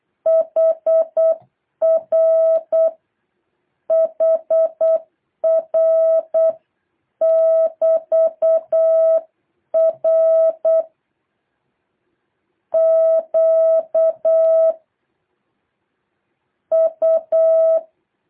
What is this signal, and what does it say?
Morse code, heard as a keyed tone, about 8 wpm, 650 Hz. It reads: HR HR <BT> R Q U
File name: jh8cbh / 2023cw